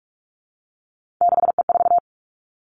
Morse code, heard as a keyed tone, about 46 wpm, 700 Hz. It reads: 6E4
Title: Three Challenging Letter-Number Combinations 45wpm